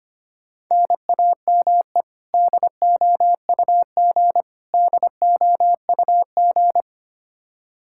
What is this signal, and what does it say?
Morse code, heard as a keyed tone, about 25 wpm, 700 Hz. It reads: NAME DOUG DOUG
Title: Practice QSO - Exchange 1 - Beginning - 25wpm